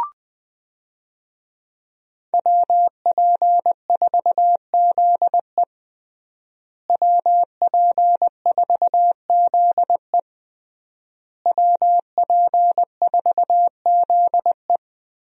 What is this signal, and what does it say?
Morse code, heard as a keyed tone, about 20 wpm, 700 Hz. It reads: WP4ZE WP4ZE WP4ZE
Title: USA Call Signs - 5 Characters - Repeated 3x - 20wpm